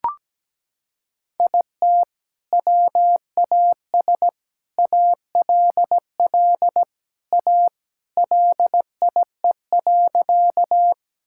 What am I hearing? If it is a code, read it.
Morse code, heard as a keyed tone, about 17 wpm, 700 Hz. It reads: IT WAS ALL A LIE.
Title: Sentences from Top 1600 Words 17wpm